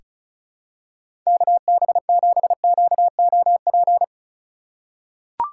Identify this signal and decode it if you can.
Morse code, heard as a keyed tone, about 36 wpm, 700 Hz. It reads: KB7QOP
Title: K1USN SST QSO Practice - 35wpm